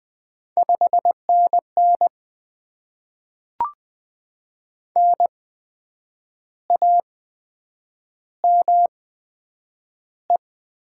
Morse code, heard as a keyed tone, about 20 wpm, spaced as for 8 wpm, 700 Hz. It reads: 5NN NAME
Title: Top 20 QSO Elements - 8x Character Spacing - 20wpm